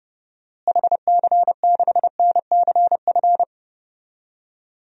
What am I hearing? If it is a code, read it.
Morse code, heard as a keyed tone, about 30 wpm, 700 Hz. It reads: HC6NCF
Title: DXCC-100 Call Signs - 6 Characters - Repeated 3x - 30wpm